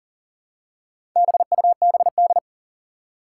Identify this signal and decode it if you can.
Morse code, heard as a keyed tone, about 40 wpm, 700 Hz. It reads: BUBD
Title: Four Challenging Letter-Number Combinations 40wpm